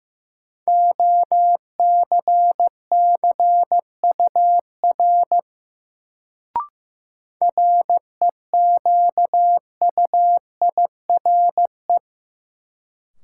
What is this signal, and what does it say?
Morse code, heard as a keyed tone, about 15 wpm, 700 Hz. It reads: OCCUR REQUIRE
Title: Sentences from Top 900 Words - New Word Drill - 15wpm